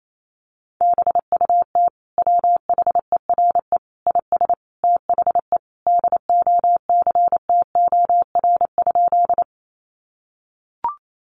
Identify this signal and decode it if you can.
Morse code, heard as a keyed tone, about 28 wpm, 700 Hz. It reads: BUT WHERE IS THE DOCTOR?
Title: Sentences from Top 1500 Words 28wpm